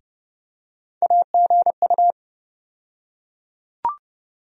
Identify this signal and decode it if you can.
Morse code, heard as a keyed tone, about 30 wpm, 700 Hz. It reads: AGU
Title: USA Call Sign Postfixes - 3 Letters 30wpm